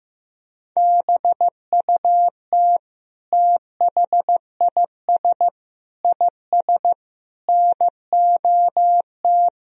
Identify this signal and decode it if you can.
Morse code, heard as a keyed tone, about 15 wpm, 700 Hz. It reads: BUT THIS IS NOT
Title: Sentences from Top 1000 Words - Review Part 2 - 15wpm